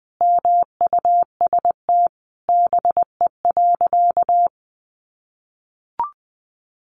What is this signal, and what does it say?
Morse code, heard as a keyed tone, about 20 wpm, 700 Hz. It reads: MUST BE.
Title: Sentences from Top 1400 Words 20wpm